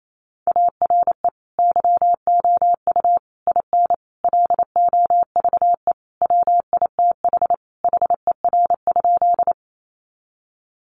Morse code, heard as a keyed tone, about 28 wpm, 700 Hz. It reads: ARE YOU IN LOVE WITH HER?
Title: Sentences from Top 400 Words 28wpm